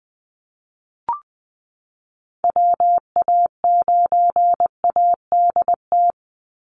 Morse code, heard as a keyed tone, about 20 wpm, 700 Hz. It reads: WA9ADT
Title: Call Signs Repeated 3x - 20wpm